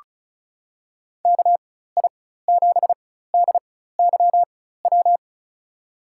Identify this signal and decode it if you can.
Morse code, heard as a keyed tone, about 34 wpm, spaced as for 16 wpm, 700 Hz. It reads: KI7DYW
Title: USA Call Signs - 6 Characters - 4x Character Spacing - 35wpm